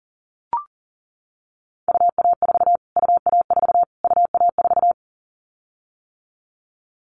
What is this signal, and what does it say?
Morse code, heard as a keyed tone, about 38 wpm, 700 Hz. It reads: UA4 UA4 UA4
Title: Callsign Prefixes for Top 25 DXCC - Repeated 3x - 40wpm